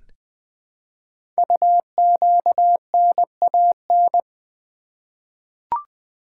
Morse code, heard as a keyed tone, about 20 wpm, 700 Hz. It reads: UQNAN
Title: Code Groups - 5 Letters - 20wpm